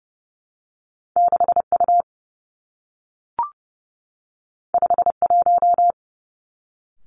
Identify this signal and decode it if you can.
Morse code, heard as a keyed tone, about 30 wpm, 700 Hz. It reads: 6 U 5 1
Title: Two Challenging Letter-Number Combinations 30wpm